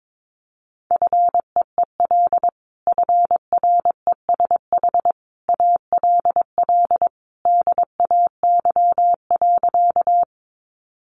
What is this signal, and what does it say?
Morse code, heard as a keyed tone, about 22 wpm, 700 Hz. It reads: FEEL FRESH ALL DAY.
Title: Sentences from Top 1200 Words 22wpm